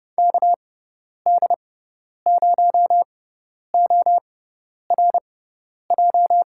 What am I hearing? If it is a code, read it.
Morse code, heard as a keyed tone, about 32 wpm, spaced as for 10 wpm, 700 Hz. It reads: KD0ORJ
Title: USA Call Signs - 6x Character Spacing - 30wpm